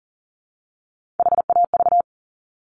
Morse code, heard as a keyed tone, about 42 wpm, 700 Hz. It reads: HAV
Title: Top Three Letter Combinations in Words 40wpm